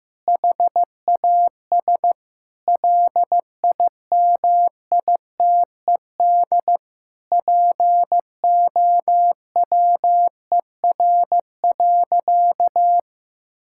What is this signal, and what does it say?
Morse code, heard as a keyed tone, about 15 wpm, 700 Hz. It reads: HAS LIMITED POWER.
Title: Sentences from Top 1000 Words - Review Part 6 - 15wpm